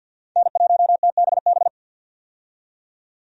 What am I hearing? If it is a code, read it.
Morse code, heard as a keyed tone, about 50 wpm, 700 Hz. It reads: N1TBB
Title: USA Call Signs - 5 Characters - 50wpm